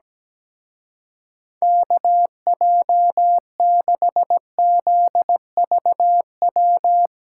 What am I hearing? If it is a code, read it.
Morse code, heard as a keyed tone, about 18 wpm, 700 Hz. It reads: KJ6ZVW
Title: Speed Racing - USA Call Signs 17wpm